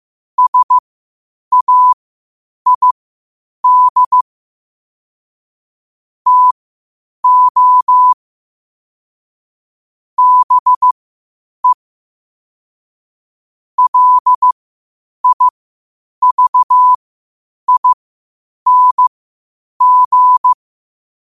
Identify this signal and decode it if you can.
Morse code, heard as a keyed tone, about 15 wpm, spaced as for 8 wpm, 1000 Hz. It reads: SAID TO BE LIVING
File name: qotdfiles / 8wpm